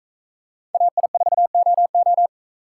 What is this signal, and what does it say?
Morse code, heard as a keyed tone, about 38 wpm, 700 Hz. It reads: AI4OO